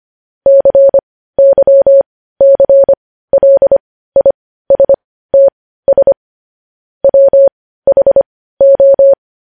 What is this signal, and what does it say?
Morse code, heard as a keyed tone, about 25 wpm, spaced as for 15 wpm, 550 Hz. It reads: CYCLISTS WHO
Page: news-headlines-in-morse-code-at-15-wpm